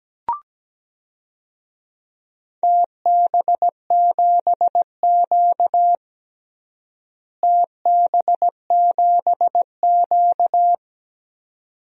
Morse code, heard as a keyed tone, about 17 wpm, 700 Hz. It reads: TB7Q TB7Q
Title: DXCC-100 Call Signs - 4 Characters - Repeated 2x - 17wpm